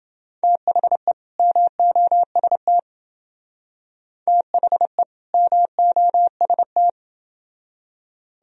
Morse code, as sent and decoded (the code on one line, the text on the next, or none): - .... / -- --- ... - / - .... / -- --- ... -
THE MOST THE MOST